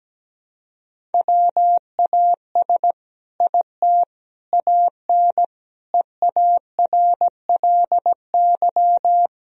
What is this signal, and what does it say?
Morse code, heard as a keyed tone, about 17 wpm, 700 Hz. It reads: WAS IT AN EARLY
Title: Sentences from Top 1000 Words - Review Part 4 - 17wpm